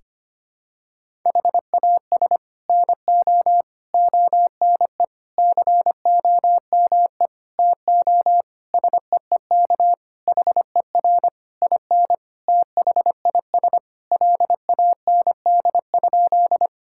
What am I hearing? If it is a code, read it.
Morse code, heard as a keyed tone, about 25 wpm, 700 Hz. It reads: HAS NO ONE COME TO SEEK HER IN THIS LAND?
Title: Sentences from Top 1600 Words 25wpm